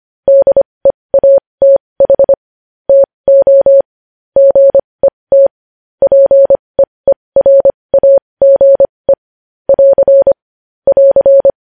Morse code, heard as a keyed tone, about 25 wpm, spaced as for 20 wpm, 550 Hz. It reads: DEATH TO GET PEERAGE <AR> <AR>
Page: News Headlines in Morse Code at 20 WPM